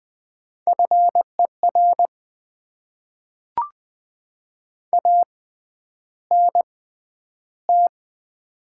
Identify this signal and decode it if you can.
Morse code, heard as a keyed tone, about 20 wpm, spaced as for 10 wpm, 690 Hz. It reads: FER ANT
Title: Top 20 QSO Elements - 6x Character Spacing - 20wpm